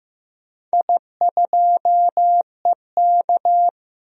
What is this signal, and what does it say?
Morse code, heard as a keyed tone, about 15 wpm, 700 Hz. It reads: I2EK
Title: DXCC-100 Call Signs - 4 Characters - 15wpm